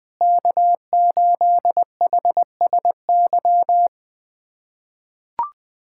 Morse code, heard as a keyed tone, about 21 wpm, 700 Hz. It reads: K8HSY
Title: K1USN SST QSO Practice - 20wpm